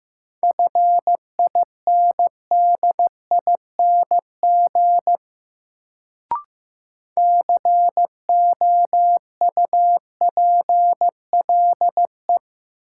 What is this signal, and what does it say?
Morse code, heard as a keyed tone, about 15 wpm, 700 Hz. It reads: FINDING COUPLE
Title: Sentences from Top 1100 Words - New Word Drill - 15wpm